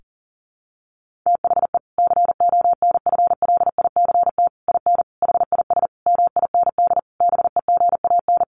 Morse code, heard as a keyed tone, about 40 wpm, 700 Hz. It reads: THE CONFLICT IN HIS MIND BEGAN
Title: Sentences from Top 1300 Words 40wpm